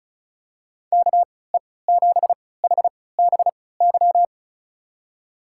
Morse code, heard as a keyed tone, about 36 wpm, spaced as for 20 wpm, 700 Hz. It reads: KE7HBY
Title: USA Call Signs - 6 Characters - 3x Character Spacing - 35wpm